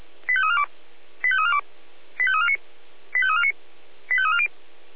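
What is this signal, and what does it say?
Five-tone selective call (ZVEI), heard as quick runs of five tones, tones of about 70 ms. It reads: 85342 85342 85349 85349 85340
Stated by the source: ZVEI five-tone selective call received by SDR, address 85360.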